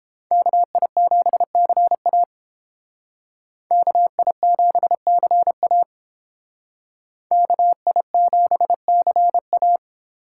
Morse code, heard as a keyed tone, about 31 wpm, 700 Hz. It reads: KI7CA KI7CA KI7CA